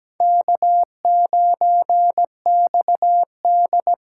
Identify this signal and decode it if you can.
Morse code, heard as a keyed tone, about 17 wpm, 700 Hz. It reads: K9XD
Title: K1USN SST QSO Practice - 17wpm